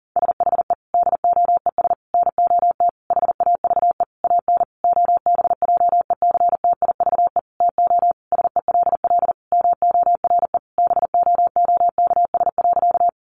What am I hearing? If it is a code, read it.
Morse code, heard as a keyed tone, about 40 wpm, 700 Hz. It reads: SHE DOES NOT HAVE AN OBJECTIVE TO SELL MORE BOOKS.
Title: Sentences from Top 1200 Words 40wpm